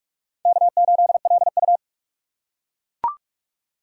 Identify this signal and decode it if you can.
Morse code, heard as a keyed tone, about 39 wpm, 700 Hz. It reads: K8LU